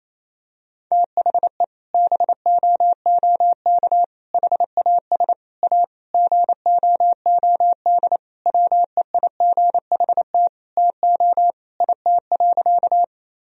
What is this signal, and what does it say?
Morse code, heard as a keyed tone, about 28 wpm, 700 Hz. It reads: THE BOOK HAS A GOOD WEIGHT TO IT.
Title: Sentences from Top 1000 Words - Review Part 3 - 28wpm